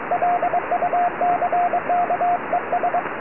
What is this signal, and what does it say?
Morse code, heard as a keyed tone, about 22 wpm, 710 Hz, with radio noise. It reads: LUCK ES
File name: tropo / sp6asd 23cm